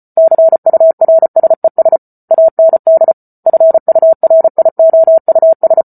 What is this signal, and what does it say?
Morse code, heard as a keyed tone, about 34 wpm, 650 Hz. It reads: CURSES AND FURIOUS